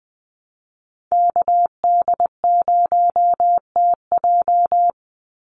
Morse code, heard as a keyed tone, about 20 wpm, 700 Hz. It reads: KD0TJ
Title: USA Call Signs 20wpm